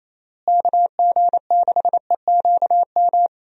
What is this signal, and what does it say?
Morse code, heard as a keyed tone, about 28 wpm, 700 Hz. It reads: KG6EQM
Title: USA Call Signs - 6 Characters - Repeated 3x - 28wpm